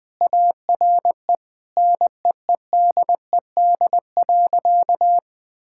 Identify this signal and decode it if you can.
Morse code, heard as a keyed tone, about 20 wpm, 700 Hz. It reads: ARE NEEDED.
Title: Sentences with 6-Word Phrases - 20wpm